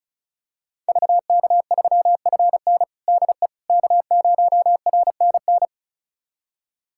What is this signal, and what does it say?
Morse code, heard as a keyed tone, about 36 wpm, 700 Hz. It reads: VK3FN DE K0RNN
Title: USA Callsign DE Callsign - 5 Characters - 35wpm